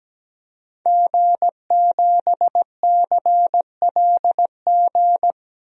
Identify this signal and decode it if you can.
Morse code, heard as a keyed tone, about 17 wpm, 700 Hz. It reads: G7CLG
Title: DXCC-100 Call Signs - 5 Characters - 17wpm